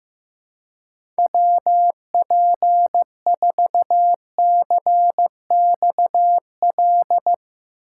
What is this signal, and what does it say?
Morse code, heard as a keyed tone, about 15 wpm, 700 Hz. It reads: WP4CXL
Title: USA Call Signs - 6 Characters - 3x Character Spacing - 15wpm